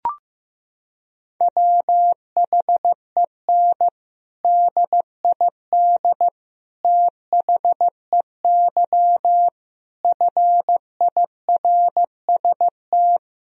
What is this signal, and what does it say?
Morse code, heard as a keyed tone, about 16 wpm, 700 Hz. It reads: WHEN DID THEY FIRST
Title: Sentences from Top 1000 Words - Review Part 2 - 15wpm